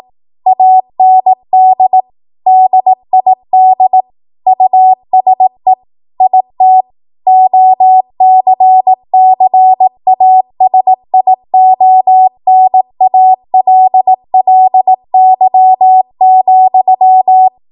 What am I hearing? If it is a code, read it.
Morse code, heard as a keyed tone, about 18 wpm, 750 Hz. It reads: AND DID USE IT OCCASIONALLY,
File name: Morse / 18WPM